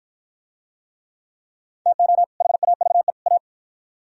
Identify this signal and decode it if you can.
Morse code, heard as a keyed tone, about 49 wpm, 700 Hz. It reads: TO HAVE A